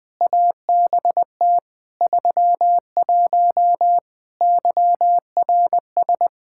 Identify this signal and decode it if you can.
Morse code, heard as a keyed tone, about 20 wpm, 700 Hz. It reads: ABT 31 YRS
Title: Practice QSO - Exchange 3 - More Info - 20wpm